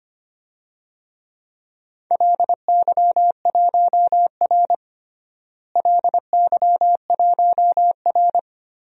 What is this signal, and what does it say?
Morse code, heard as a keyed tone, about 25 wpm, 700 Hz. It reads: LY1R LY1R
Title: DXCC-100 Call Signs - 4 Characters - Repeated 2x - 25wpm